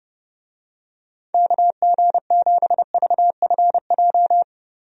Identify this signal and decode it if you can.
Morse code, heard as a keyed tone, about 30 wpm, 700 Hz. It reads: KG7VFJ
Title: USA Call Signs - 6 Characters - Repeated 3x - 30wpm